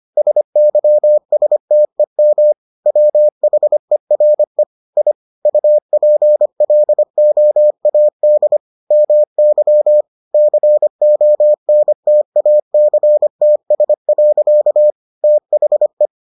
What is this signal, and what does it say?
Morse code, heard as a keyed tone, about 25 wpm, 600 Hz. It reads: SYSTEM WHERE I UPLOAD MY CONTACTS. THE